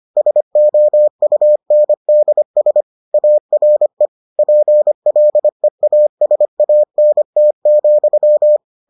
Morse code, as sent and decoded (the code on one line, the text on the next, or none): ... --- ..- -. -.. ... / .- .-. / .--. .-.. .- ... .- -. - --..--
SOUNDS ARE PLEASANT,